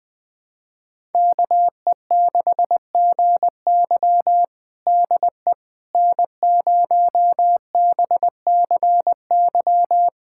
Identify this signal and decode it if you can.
Morse code, heard as a keyed tone, about 20 wpm, 700 Hz. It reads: KE6GY DE N0BCY
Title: USA Callsign DE Callsign 20wpm